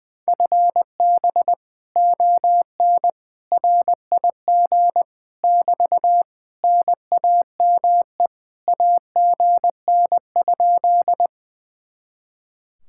Morse code, heard as a keyed tone, about 20 wpm, 700 Hz. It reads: FB ON RIG <BT> NAME AGN?
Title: Speed-Racing Common QSO Phrases - 20wpm